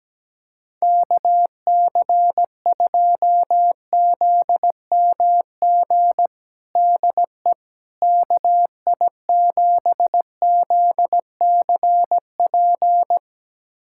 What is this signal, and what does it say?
Morse code, heard as a keyed tone, about 17 wpm, 700 Hz. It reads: KC2ZMG DE KI7ZCP
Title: USA Callsign DE Callsign - 6 Characters - 17wpm